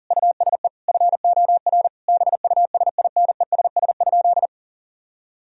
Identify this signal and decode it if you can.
Morse code, heard as a keyed tone, about 37 wpm, 700 Hz. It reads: USE FOR BUSINESS?
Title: Sentences from Top 1000 Words - Review Part 3 - 40wpm